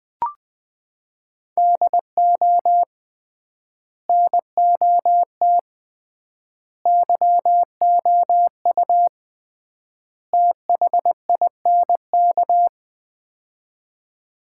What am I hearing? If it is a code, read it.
Morse code, heard as a keyed tone, about 20 wpm, 700 Hz. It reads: DO NOT YOU THINK
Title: 3x Word-Spacing - Sets of 4 Words - 20wpm